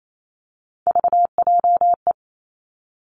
Morse code, heard as a keyed tone, about 28 wpm, 700 Hz. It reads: VJE
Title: Three Challenging Letter-Number Combinations 28wpm